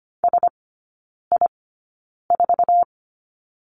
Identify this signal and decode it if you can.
Morse code, heard as a keyed tone, about 25 wpm, spaced as for 9 wpm, 700 Hz. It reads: SI4
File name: Lesson 17e - Letter u - Three Characters 25wpm